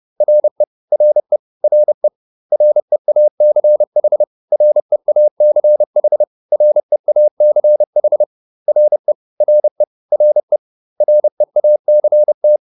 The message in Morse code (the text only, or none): RE RE RE REACH REACH REACH RE RE RE REACT